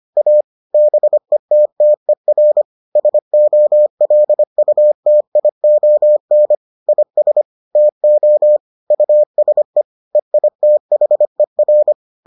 Morse code, heard as a keyed tone, about 25 wpm, 600 Hz. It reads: A BETTER SOLUTION IS TO USE EITHER